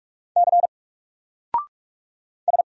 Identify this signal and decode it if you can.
Morse code, heard as a keyed tone, about 44 wpm, 700 Hz. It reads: C S
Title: Single Letter-Number 45wpm